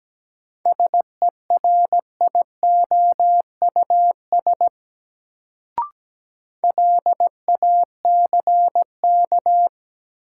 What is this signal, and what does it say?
Morse code, heard as a keyed tone, about 17 wpm, 700 Hz. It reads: SERIOUS LACK
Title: Sentences from Top 1100 Words - New Word Drill Alt - 17wpm